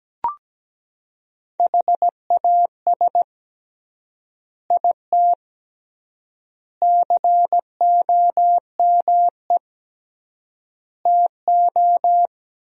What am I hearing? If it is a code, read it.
Morse code, heard as a keyed tone, about 17 wpm, 700 Hz. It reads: HAS IT COME TO
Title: 3x Word-Spacing - Sentences from Top 100 Words - 17wpm